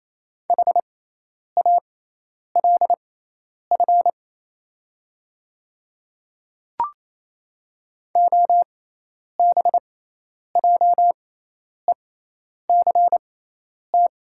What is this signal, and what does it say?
Morse code, heard as a keyed tone, about 28 wpm, spaced as for 10 wpm, 690 Hz. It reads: HALF OBJECT
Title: 6x Character Spacing - Top 500 Words - 28wpm